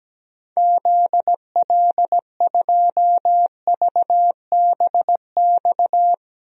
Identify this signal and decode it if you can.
Morse code, heard as a keyed tone, about 17 wpm, 700 Hz. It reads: ZL2VBX
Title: DXCC-100 Call Signs - 6 Characters - Repeated 2x - 17wpm